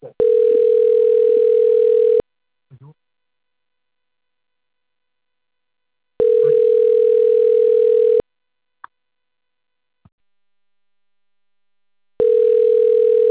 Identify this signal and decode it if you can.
Ringback tone (North America), 440 + 480 Hz.
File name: out- - - - - .55759